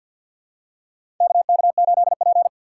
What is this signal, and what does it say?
Morse code, heard as a keyed tone, about 52 wpm, 700 Hz. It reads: KK8P
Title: USA Call Signs - 4 Characters - 6x Character Spacing - 50wpm